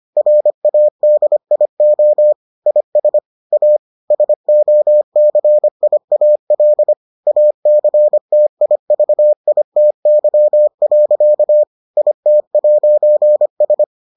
Morse code, heard as a keyed tone, about 25 wpm, 600 Hz. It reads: RADIO IS A SOCIAL ACTIVITY. IT'S